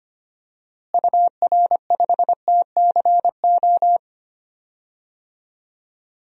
Morse code, heard as a keyed tone, about 25 wpm, 700 Hz. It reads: UR5TCO
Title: DXCC-100 Call Signs - 6 Characters - Repeated 2x - 25wpm